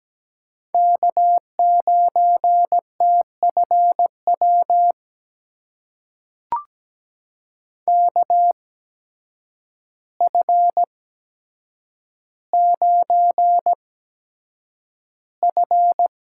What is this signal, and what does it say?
Morse code, heard as a keyed tone, about 17 wpm, spaced as for 8 wpm, 700 Hz. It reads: K9TFW KF9F
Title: USA Call Signs - 5 Characters - 8x Character Spacing - 17wpm